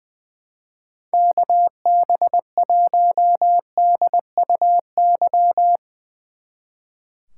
Morse code, heard as a keyed tone, about 22 wpm, 700 Hz. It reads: KB1DUY
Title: Speed Racing - USA 2x3 Call Signs 20wpm